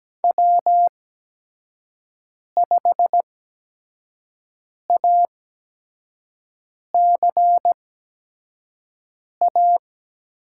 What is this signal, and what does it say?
Morse code, heard as a keyed tone, about 17 wpm, spaced as for 5 wpm, 700 Hz. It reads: W5ACA
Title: USA Call Signs - 8x Character Spacing - 17wpm